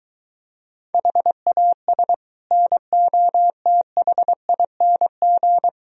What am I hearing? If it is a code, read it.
Morse code, heard as a keyed tone, about 22 wpm, 700 Hz. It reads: HAS NOTHING